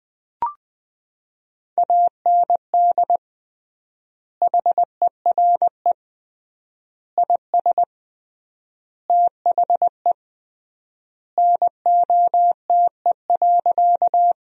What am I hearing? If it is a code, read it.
Morse code, heard as a keyed tone, about 20 wpm, 700 Hz. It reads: AND HERE IS THE NOTE.
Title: 3x Word-Spacing - Sentences from Top 500 Words - 20wpm